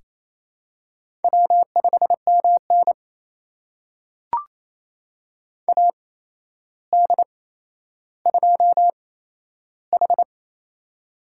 Morse code, heard as a keyed tone, about 27 wpm, spaced as for 12 wpm, 700 Hz. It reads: W5MN AD2H
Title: USA Call Signs - 4 Characters - 8x Character Spacing - 28wpm